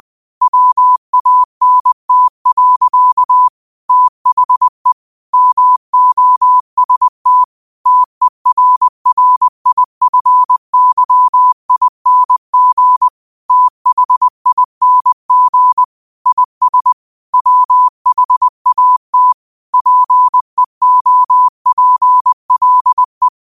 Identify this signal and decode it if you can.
Morse code, heard as a keyed tone, about 20 wpm, 1000 Hz. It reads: WANT. THE MOST TERRIFYING THING IS WHAT PEOPLE